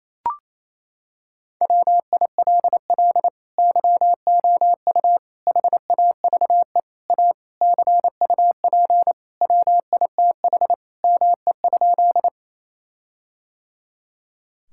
Morse code, heard as a keyed tone, about 28 wpm, 700 Hz. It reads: WILL YOU HAVE A CUP WITH ME?
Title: Sentences from Top 1600 Words 28wpm